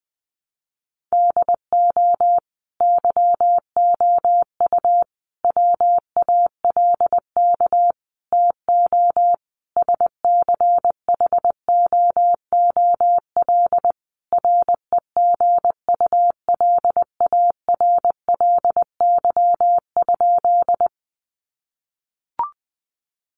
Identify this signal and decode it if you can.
Morse code, heard as a keyed tone, about 20 wpm, 700 Hz. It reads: DO YOU WALK TO SCHOOL REGULARLY?
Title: Sentences from Top 1400 Words 20wpm